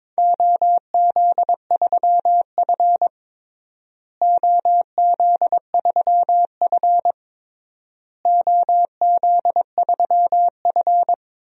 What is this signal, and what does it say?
Morse code, heard as a keyed tone, about 22 wpm, 700 Hz. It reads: OZ3F OZ3F OZ3F
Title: DXCC-100 Call Signs - 4 Characters - Repeated 3x - 22wpm